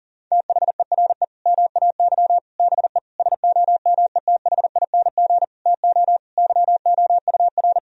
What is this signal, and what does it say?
Morse code, heard as a keyed tone, about 40 wpm, 700 Hz. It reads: THERE MAY BE SOMETHING TO YOUR
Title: Sentences from Top 1000 Words - Review Part 3 - 40wpm